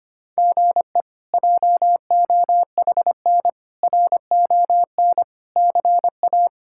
Morse code, heard as a keyed tone, about 25 wpm, 700 Hz. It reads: GE JOHN RON CA
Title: K1USN SST QSO Practice - 25wpm